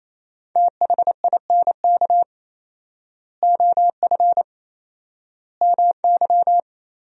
Morse code, heard as a keyed tone, about 28 wpm, 700 Hz. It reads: THINK OF MY
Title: 4x Word-Spacing - Sentences from Top 300 Words - 28wpm